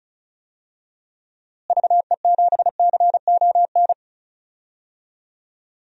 Morse code, heard as a keyed tone, about 34 wpm, 700 Hz. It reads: VE7CON
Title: DXCC-100 Call Signs - 6 Characters - Repeated 2x - 35wpm